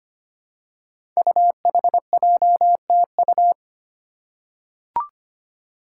Morse code, heard as a keyed tone, about 25 wpm, 700 Hz. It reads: UHJTU
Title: Five Challenging Letter-Number Combinations 25wpm